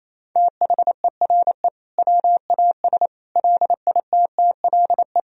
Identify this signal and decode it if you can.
Morse code, heard as a keyed tone, about 28 wpm, 700 Hz. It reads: THERE WAS LITTLE
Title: Sentences from Top 1000 Words - Review Part 4 - 28wpm